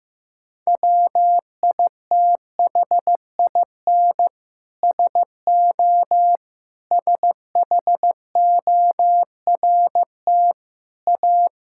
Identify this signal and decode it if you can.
Morse code, heard as a keyed tone, about 16 wpm, 700 Hz. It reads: WITHIN SO SHORT A
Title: Sentences from Top 1000 Words - Review Part 5 - 15wpm